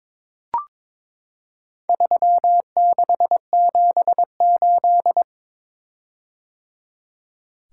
Morse code, heard as a keyed tone, about 22 wpm, 700 Hz. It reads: 3678
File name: practice / 22wpm / Four Letter-Number Combinations - 22wpm